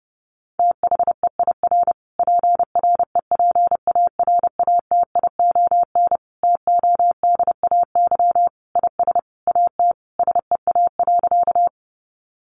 Morse code, heard as a keyed tone, about 30 wpm, 700 Hz. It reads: THEIR PREPARATION TODAY IS AT SEA.
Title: Sentences from Top 1400 Words 30wpm